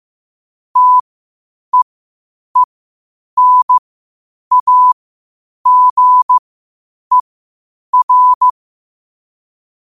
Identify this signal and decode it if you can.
Morse code, heard as a keyed tone, about 15 wpm, spaced as for 8 wpm, 1000 Hz. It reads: TEENAGER
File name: qotdfiles / 8wpm